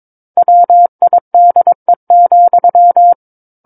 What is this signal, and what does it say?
Morse code, heard as a keyed tone, about 22 wpm, 700 Hz. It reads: WIDE,
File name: Book One / 13-Chapter 13 - 22 WPM